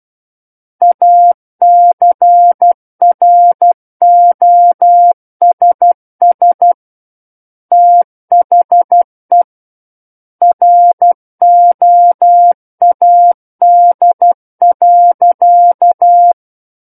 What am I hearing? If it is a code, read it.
Morse code, heard as a keyed tone, about 12 wpm, 700 Hz. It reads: ACROSS THE ROAD.